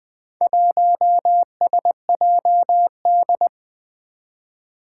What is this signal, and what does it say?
Morse code, heard as a keyed tone, about 20 wpm, 700 Hz. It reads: 1SJD
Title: Four Challenging Letter-Number Combinations 20wpm